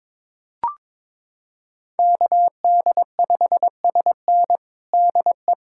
Morse code, heard as a keyed tone, about 22 wpm, 700 Hz. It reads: KD5SN DE
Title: USA Callsign DE Callsign 22wpm